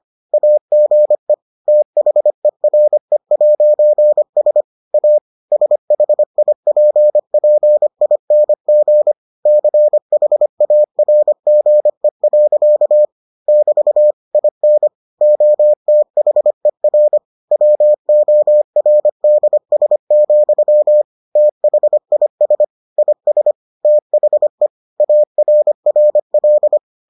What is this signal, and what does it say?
Morse code, heard as a keyed tone, about 25 wpm, 600 Hz. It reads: AGE THERE'S A SHIPPING CHARGE. <BT> IN OTHER WORDS, THIS IS THE ARRL